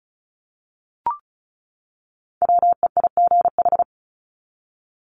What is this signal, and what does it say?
Morse code, heard as a keyed tone, about 34 wpm, 700 Hz. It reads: WEIGH
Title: Top 1751-2000 Words 35wpm